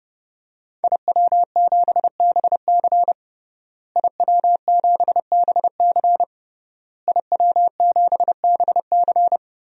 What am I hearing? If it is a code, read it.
Morse code, heard as a keyed tone, about 30 wpm, 700 Hz. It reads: IW7BC IW7BC IW7BC
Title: DXCC-100 Call Signs - 5 Characters - Repeated 3x - 30wpm